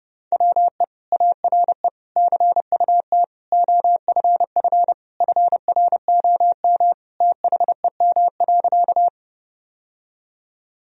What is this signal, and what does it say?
Morse code, heard as a keyed tone, about 32 wpm, 700 Hz. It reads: WE ARE CUT OFF FROM THEM.
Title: Sentences from Top 1000 Words - Review Part 1 - 30wpm